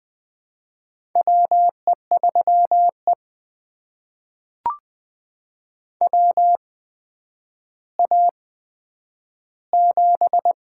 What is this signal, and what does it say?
Morse code, heard as a keyed tone, about 20 wpm, spaced as for 10 wpm, 700 Hz. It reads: WE3E WA7
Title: USA Call Signs - 4 Characters - 8x Character Spacing - 20wpm